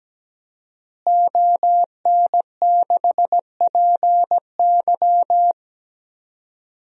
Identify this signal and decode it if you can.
Morse code, heard as a keyed tone, about 17 wpm, 700 Hz. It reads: ON6PY
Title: DXCC-100 Call Signs - 5 Characters - Repeated 2x - 17wpm